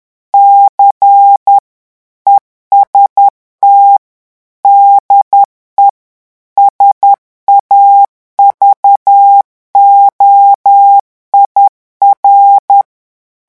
Morse code, heard as a keyed tone, about 10 wpm, 800 Hz. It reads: C EST DE SAVOIR
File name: morse mp3 / revision francais 10 mots minute